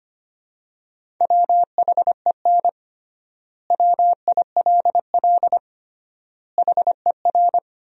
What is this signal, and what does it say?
Morse code, heard as a keyed tone, about 25 wpm, 700 Hz. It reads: WHEN WILL HER
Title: 3x Word-Spacing - Sentences from Top 200 Words - 25wpm